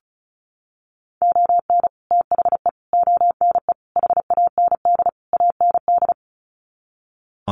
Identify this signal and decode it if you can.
Morse code, heard as a keyed tone, about 34 wpm, 700 Hz. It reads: ON THE ONE HAND AND
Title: Sets of 5 Words Encore2 35wpm